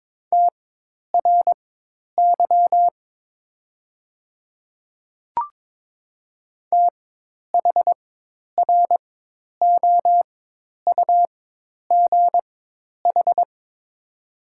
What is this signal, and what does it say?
Morse code, heard as a keyed tone, about 22 wpm, spaced as for 10 wpm, 700 Hz. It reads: TRY THROUGH
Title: 4x Character Spacing - Top 300 Words - 22wpm